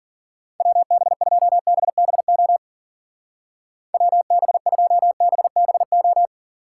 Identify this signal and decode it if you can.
Morse code, heard as a keyed tone, about 50 wpm, 700 Hz. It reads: WB2BBO WB2BBO
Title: Speed Racing - USA 2x3 Call Signs 40wpm